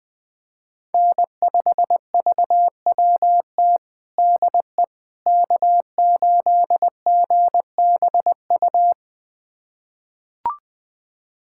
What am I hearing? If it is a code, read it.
Morse code, heard as a keyed tone, about 21 wpm, 700 Hz. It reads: N5VWT DE K8GBU
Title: USA Callsign DE Callsign - 5 Characters - 20wpm